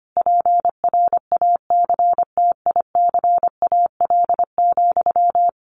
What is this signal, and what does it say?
Morse code, heard as a keyed tone, about 25 wpm, 700 Hz. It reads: PRACTICAL,
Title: Sentences from Top 1400 Words 25wpm